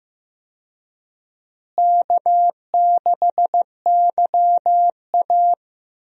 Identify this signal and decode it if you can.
Morse code, heard as a keyed tone, about 15 wpm, 700 Hz. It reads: K6YA
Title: USA Call Signs - 4 Characters - Repeated 2x - 15wpm